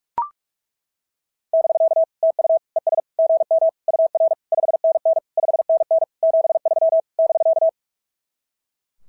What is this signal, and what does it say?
Morse code, heard as a keyed tone, about 42 wpm, 660 Hz. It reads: <BK> TU ES GM UR 5NN 5NN 73 <BK>
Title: SOTA QSO Practice 45wpm